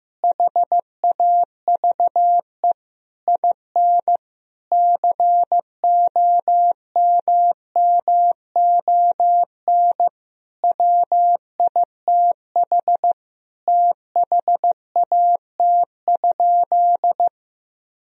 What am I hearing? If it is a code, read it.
Morse code, heard as a keyed tone, about 15 wpm, 700 Hz. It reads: HAVE IN COMMON WITH THAT?
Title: Sentences from Top 1000 Words - Review Part 4 - 15wpm